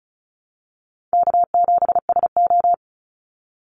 Morse code, heard as a keyed tone, about 36 wpm, 700 Hz. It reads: K7SO
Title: NA SOTA Activator Callsigns 35wpm